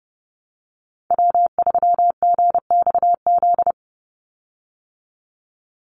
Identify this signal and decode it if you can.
Morse code, heard as a keyed tone, about 30 wpm, 700 Hz. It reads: W3GXZ
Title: USA Call Signs 30wpm